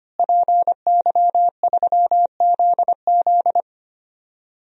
Morse code, heard as a keyed tone, about 25 wpm, 700 Hz. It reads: PY3ZZ
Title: DXCC-100 Call Signs - 5 Characters - Repeated 2x - 25wpm